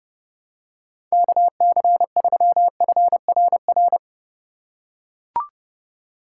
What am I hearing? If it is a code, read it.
Morse code, heard as a keyed tone, about 28 wpm, 700 Hz. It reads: KC3FRR
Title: USA Call Signs - 3x Character Spacing - 30wpm